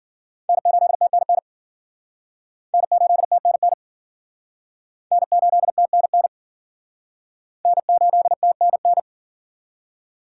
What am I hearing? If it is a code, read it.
Morse code, heard as a keyed tone, about 50 wpm, 700 Hz. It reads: N8TNN N8TNN N8TNN N8TNN
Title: Speed Racing - USA 1x3 Call Signs 40wpm